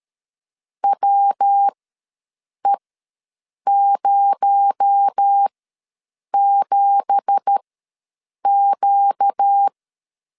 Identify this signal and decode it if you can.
Morse code, heard as a keyed tone, about 13 wpm, spaced as for 7 wpm, 800 Hz. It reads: WE07Q
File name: LEZIONI / 10 SIGNS / 10D COMA ws 12wpm 3618